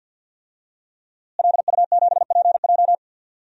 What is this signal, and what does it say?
Morse code, heard as a keyed tone, about 52 wpm, 700 Hz. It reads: LU7PJ
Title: DXCC-100 Call Signs - 5 Characters - Repeated 3x - 50wpm